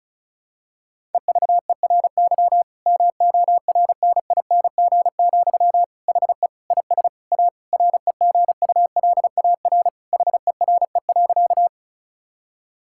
Morse code, heard as a keyed tone, about 36 wpm, 700 Hz. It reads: EVERY MORNING, HE IS A REGULAR HERE.
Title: Sentences from Top 1000 Words - Review Part 6 - 35wpm